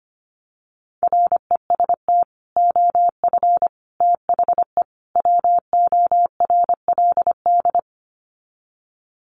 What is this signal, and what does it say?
Morse code, heard as a keyed tone, about 25 wpm, 700 Hz. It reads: REST OF THE WORLD
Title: Sets of 4 Words 25wpm